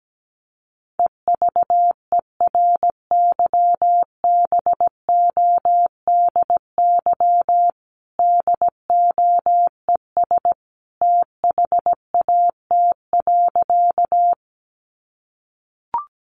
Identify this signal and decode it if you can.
Morse code, heard as a keyed tone, about 17 wpm, 700 Hz. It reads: EVERYBODY DOES THAT.
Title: Sentences from Top 1400 Words 17wpm